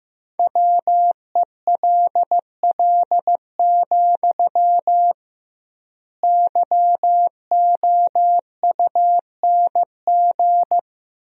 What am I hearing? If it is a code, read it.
Morse code, heard as a keyed tone, about 15 wpm, 700 Hz. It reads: WELL, YOUNG
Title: 2x Word-Spacing - Sentences from Top 400 Words - 15wpm